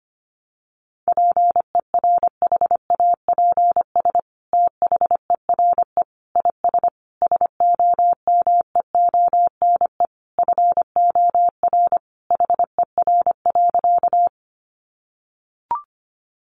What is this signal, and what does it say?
Morse code, heard as a keyed tone, about 25 wpm, 700 Hz. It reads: PERHAPS THERE IS SOMEONE FOR HER.
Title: Sentences from Top 600 Words 25wpm